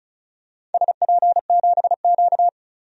Morse code, heard as a keyed tone, about 36 wpm, 700 Hz. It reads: SP7Q
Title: DXCC-100 Call Signs - 4 Characters - 35wpm